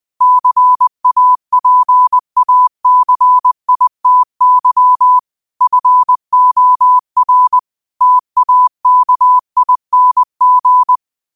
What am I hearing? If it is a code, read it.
Morse code, heard as a keyed tone, about 20 wpm, 1000 Hz. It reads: CAPACITY FOR TAKING
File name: qotdfiles / 20wpm